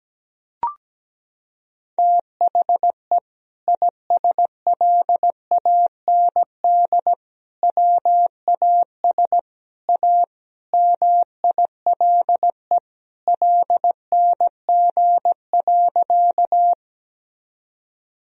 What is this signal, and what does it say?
Morse code, heard as a keyed tone, about 17 wpm, 700 Hz. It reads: THE ISLAND WAS A MILE LNG.
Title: Sentences from Top 500 Words and Abbreviations 17wpm